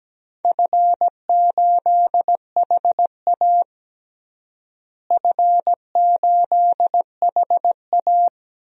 Morse code, heard as a keyed tone, about 17 wpm, 700 Hz. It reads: F8HA F8HA
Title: DXCC-100 Call Signs - 4 Characters - Repeated 2x - 17wpm